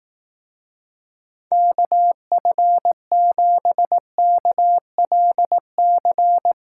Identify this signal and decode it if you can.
Morse code, heard as a keyed tone, about 18 wpm, 700 Hz. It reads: KF7KLC
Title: Speed Racing - USA 2x3 Call Signs 15wpm